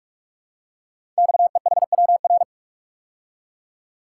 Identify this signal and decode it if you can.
Morse code, heard as a keyed tone, about 44 wpm, 690 Hz. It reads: XEHWR
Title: Code Groups - 5 Letters - 45wpm